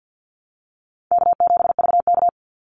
Morse code, heard as a keyed tone, about 49 wpm, 700 Hz. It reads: K7VK
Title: NA SOTA Chaser Callsigns 50wpm